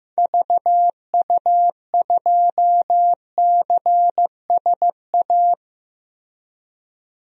Morse code, heard as a keyed tone, about 15 wpm, 700 Hz. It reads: VU2CSA
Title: DXCC-100 Call Signs - 6 Characters - Repeated 2x - 15wpm